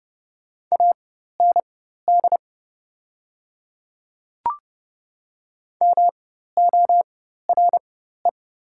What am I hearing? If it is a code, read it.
Morse code, heard as a keyed tone, about 30 wpm, spaced as for 14 wpm, 690 Hz. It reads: AND MORE
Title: 4x Character Spacing - Top 100 Words - 30wpm